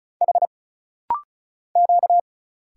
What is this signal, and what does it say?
Morse code, heard as a keyed tone, about 34 wpm, 700 Hz. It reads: H Q
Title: Single Letters - Rapid-Fire 35wpm